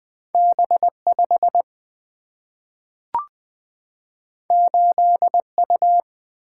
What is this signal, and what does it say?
Morse code, heard as a keyed tone, about 20 wpm, 700 Hz. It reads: B 5 8 U
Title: Single-Letter Single-Number 20wpm